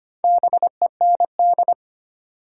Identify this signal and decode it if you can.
Morse code, heard as a keyed tone, about 25 wpm, 700 Hz. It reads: BEND
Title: Four Letter Words - 25wpm